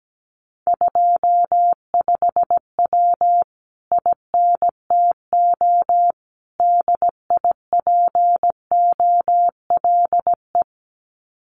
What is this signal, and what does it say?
Morse code, heard as a keyed tone, about 17 wpm, 700 Hz. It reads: 25W INTO DIPOLE
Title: Common QSO Phrases Encore 17wpm